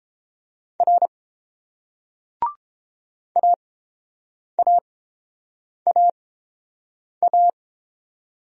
Morse code, heard as a keyed tone, about 30 wpm, spaced as for 7 wpm, 700 Hz. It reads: R AAAA